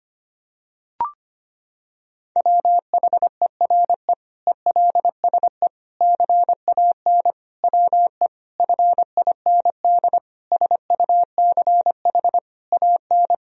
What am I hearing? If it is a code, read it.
Morse code, heard as a keyed tone, about 25 wpm, 700 Hz. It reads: WHERE ELSE CAN WE FIND SUCH AN
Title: Sentences from Top 1000 Words - Review Part 1 - 25wpm